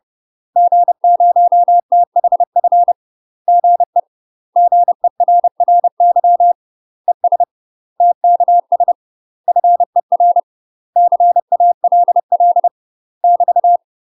Morse code, heard as a keyed tone, about 30 wpm, 700 Hz. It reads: G0THF GE GERRY ES TKS FER CALL <BT>
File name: UK MASTER QSOs TRAINING / 30WPM / UK MASTER QSOs 0048